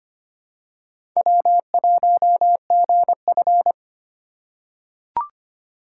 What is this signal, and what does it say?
Morse code, heard as a keyed tone, about 25 wpm, 700 Hz. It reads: W1GF
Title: USA Call Signs - 4 Characters - 3x Character Spacing - 25wpm